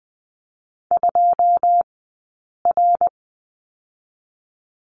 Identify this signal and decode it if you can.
Morse code, heard as a keyed tone, about 20 wpm, 700 Hz. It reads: 2 R
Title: 099 Lesson 17d - Letter u - Two Characters 20wpm